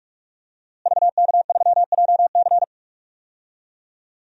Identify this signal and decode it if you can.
Morse code, heard as a keyed tone, about 42 wpm, 700 Hz. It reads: VK3JC
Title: DXCC-100 Call Signs - 5 Characters - Repeated 2x - 45wpm